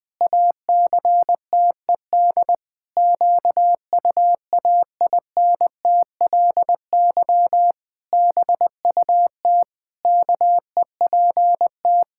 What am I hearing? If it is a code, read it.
Morse code, heard as a keyed tone, about 20 wpm, 700 Hz. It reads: ACTED QUAINTLY BUT KEPT
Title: Pangram Challenge v2 - 20wpm